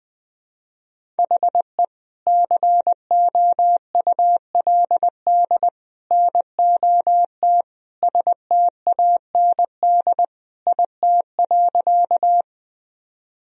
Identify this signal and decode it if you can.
Morse code, heard as a keyed tone, about 21 wpm, 700 Hz. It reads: HE COULD NOT STAND IT.
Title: Sentences from Top 1000 Words - Review Part 1 - 20wpm